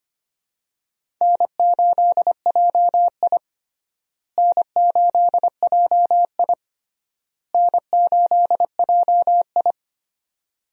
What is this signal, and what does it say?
Morse code, heard as a keyed tone, about 25 wpm, 700 Hz. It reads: N8JI N8JI N8JI
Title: USA Call Signs - 4 Characters - Repeated 3x - 25wpm